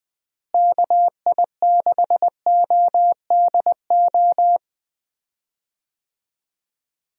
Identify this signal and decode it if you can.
Morse code, heard as a keyed tone, about 20 wpm, 700 Hz. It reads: KI6ODO
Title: USA Call Signs - 6 Characters - Repeated 3x - 20wpm